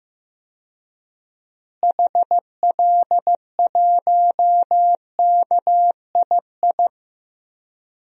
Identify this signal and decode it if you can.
Morse code, heard as a keyed tone, about 15 wpm, 700 Hz. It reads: HL1KII
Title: DXCC-100 Call Signs - 6 Characters - Repeated 2x - 15wpm